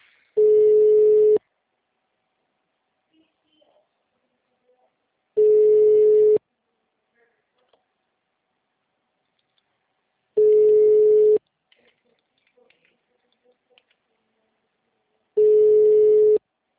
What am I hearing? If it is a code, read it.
Ringback tone (Europe, 425 Hz), with speech.